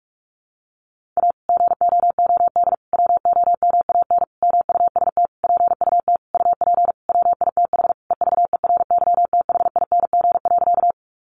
Morse code, heard as a keyed tone, about 46 wpm, 700 Hz. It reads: A GOOD WOMAN MUST PUT UP WITH EVERYTHING.
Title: Sentences from Top 1100 Words 45wpm